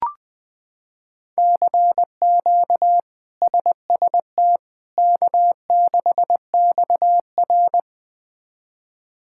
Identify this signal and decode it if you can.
Morse code, heard as a keyed tone, about 21 wpm, 700 Hz. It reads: CQ SST K6XR
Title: K1USN SST QSO Practice - 20wpm